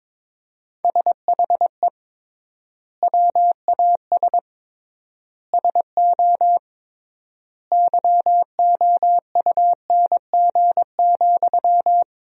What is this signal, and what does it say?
Morse code, heard as a keyed tone, about 22 wpm, 700 Hz. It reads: SHE WAS SO YOUNG,
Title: 3x Word-Spacing - Sentences from Top 400 Words - 22wpm